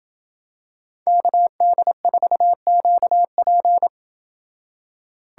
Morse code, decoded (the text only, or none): KD4QP